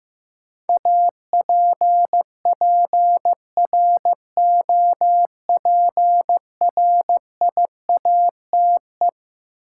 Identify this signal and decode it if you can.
Morse code, heard as a keyed tone, about 15 wpm, 700 Hz. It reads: APPROPRIATE
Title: Sentences from Top 900 Words - New Word Drill - 15wpm